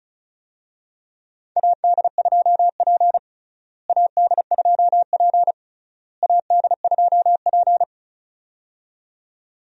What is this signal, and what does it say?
Morse code, heard as a keyed tone, about 36 wpm, 700 Hz. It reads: AD2P AD2P AD2P
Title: USA Call Signs - 4 Characters - Repeated 3x - 35wpm